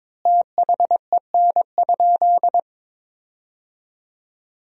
Morse code, heard as a keyed tone, about 22 wpm, 700 Hz. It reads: THEN?
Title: 4x Word-Spacing - Sentences from Top 100 Words - 22wpm